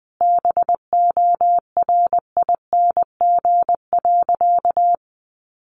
Morse code, heard as a keyed tone, about 20 wpm, 700 Hz. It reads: BORING.
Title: Easy Phrases 20wpm